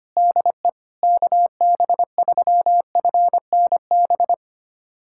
Morse code, heard as a keyed tone, about 25 wpm, 700 Hz. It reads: DE KB3FNB
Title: USA Callsign DE Callsign - 6 Characters - 25wpm